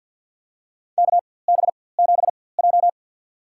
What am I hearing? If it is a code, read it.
Morse code, heard as a keyed tone, about 53 wpm, spaced as for 24 wpm, 690 Hz. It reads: KB7J
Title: USA Call Signs - 4x Character Spacing - 50wpm